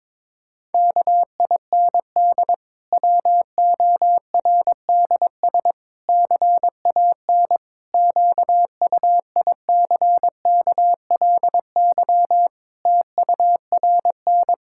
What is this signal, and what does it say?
Morse code, heard as a keyed tone, about 22 wpm, 700 Hz. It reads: KIND WORDS CAN QUICKLY TURN
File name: Difficult Sentences with Hints - 22wpm